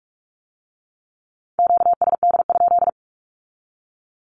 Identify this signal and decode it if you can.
Morse code, heard as a keyed tone, about 46 wpm, 700 Hz. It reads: QSD?
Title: Q-Codes with Spoken Meaning - 45wpm